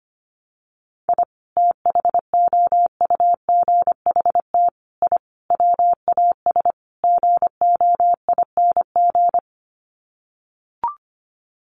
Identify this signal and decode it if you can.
Morse code, heard as a keyed tone, about 25 wpm, 700 Hz. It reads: I THOUGHT I WAS GOING
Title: Sets of 5 Words Encore2 25wpm